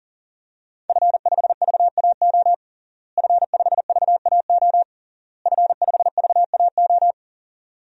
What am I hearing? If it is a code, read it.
Morse code, heard as a keyed tone, about 40 wpm, 700 Hz. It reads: F5VAO F5VAO F5VAO
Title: DXCC-100 Call Signs - 5 Characters - Repeated 3x - 40wpm